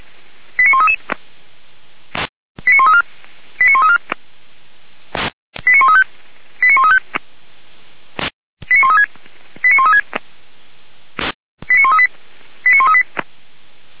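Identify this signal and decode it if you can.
Five-tone selective call (ZVEI), heard as quick runs of five tones, tones of about 70 ms. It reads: 89133 89134 89134 89135 89135 89136 89136 89138 89138
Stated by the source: ZVEI five-tone selective call received by SDR, logged by feuerwehr-orlamuende.org fire brigade server.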